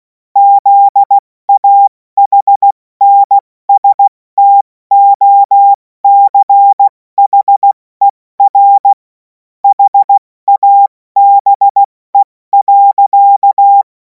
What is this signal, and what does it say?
Morse code, heard as a keyed tone, about 16 wpm, 800 Hz. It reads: ZAHNSTOCHER HABE.
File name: Morsekurs / Witze / Witz 17. März 2026 70 F80